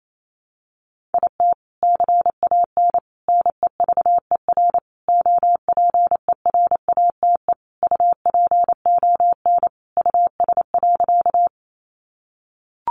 Morse code, heard as a keyed tone, about 28 wpm, 700 Hz. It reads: IT CAN NEVER OPERATE UPON US.
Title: Sentences from Top 1500 Words 28wpm